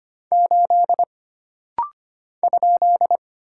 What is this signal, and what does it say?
Morse code, heard as a keyed tone, about 25 wpm, 700 Hz. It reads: 8 ?
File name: Single Character (Letter:Number:Punctuation:Prosign) - Rapid-Fire - 25wpm